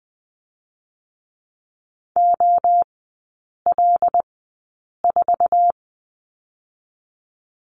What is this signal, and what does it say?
Morse code, heard as a keyed tone, about 20 wpm, spaced as for 8 wpm, 700 Hz. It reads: OL4
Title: 138 Lesson 23e - Letter f - Three Characters 20wpm